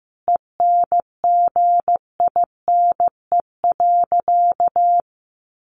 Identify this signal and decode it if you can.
Morse code, heard as a keyed tone, about 15 wpm, 700 Hz. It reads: ENGINE.
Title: Sentences from Top 1500 Words 15wpm